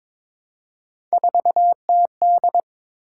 Morse code, heard as a keyed tone, about 22 wpm, 700 Hz. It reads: 4TD
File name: practice / 22wpm / Three Letter-Number Combinations - 22wpm